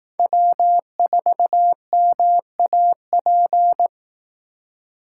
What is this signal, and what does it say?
Morse code, heard as a keyed tone, about 18 wpm, 700 Hz. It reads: W4MAP
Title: Speed Racing - USA 1x3 Call Signs 15wpm